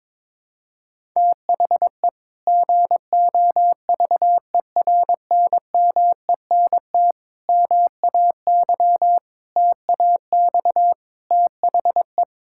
Morse code, heard as a keyed tone, about 21 wpm, 700 Hz. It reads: THE GOVERNMENT MAY TAX THE
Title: Sentences from Top 1000 Words - Review Part 5 - 22wpm